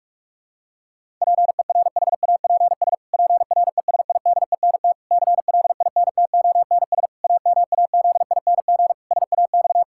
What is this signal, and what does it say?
Morse code, heard as a keyed tone, about 44 wpm, 700 Hz. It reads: PERHAPS PRESIDENT CLINTONS AMAZING SAX
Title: Pangram Challenge v2 - 45wpm